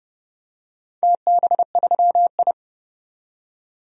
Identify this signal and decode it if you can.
Morse code, heard as a keyed tone, about 30 wpm, 700 Hz. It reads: TB3I
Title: DXCC-100 Call Signs - 4 Characters - 30wpm